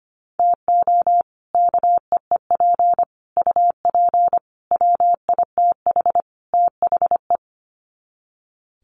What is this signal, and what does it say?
Morse code, heard as a keyed tone, about 25 wpm, 700 Hz. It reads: TO KEEP UP WITH THE
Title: Sets of 5 Words Encore 25wpm